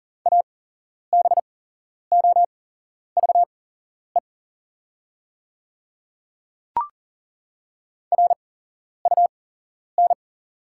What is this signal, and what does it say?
Morse code, heard as a keyed tone, about 40 wpm, spaced as for 11 wpm, 690 Hz. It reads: ABOVE RUN
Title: 8x Character Spacing - Top 500 Words - 40wpm